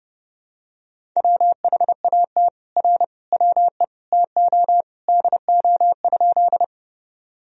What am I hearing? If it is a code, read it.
Morse code, heard as a keyed tone, about 32 wpm, 700 Hz. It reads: WHAT R WE TO DO?
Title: Sentences from Top 100 Words and Abbreviations 30wpm